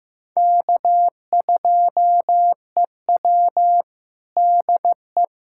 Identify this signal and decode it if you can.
Morse code, heard as a keyed tone, about 15 wpm, 700 Hz. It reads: K2EW DE
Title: USA Callsign DE Callsign - 4 Characters - 15wpm